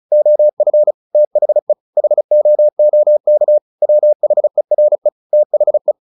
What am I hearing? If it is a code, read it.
Morse code, heard as a keyed tone, about 34 wpm, 600 Hz. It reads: OF THE HOOK WHERE THE
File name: the old man and the sea / the old man and the sea 1 35wpm